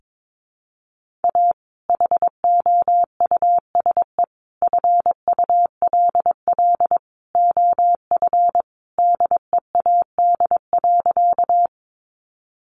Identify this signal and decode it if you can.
Morse code, heard as a keyed tone, about 22 wpm, 700 Hz. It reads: A HOUSE FULL OF DEAD.
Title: Sentences from Top 1500 Words 22wpm